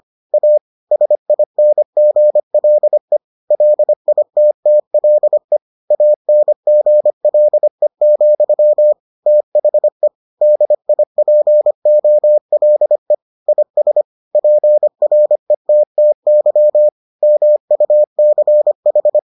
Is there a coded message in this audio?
Yes: Morse code from 0.3 to 19.2 s, heard as a keyed tone, about 25 wpm, 600 Hz. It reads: A SINGLE LITTLE ANGLE, THE DIPOLE IS PRETTY MUCH